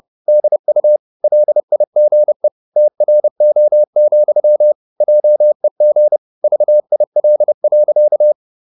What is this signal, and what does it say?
Morse code, heard as a keyed tone, about 30 wpm, 600 Hz. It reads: DU LIGE TRO, JEG VIL.
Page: Ferie 30 wpm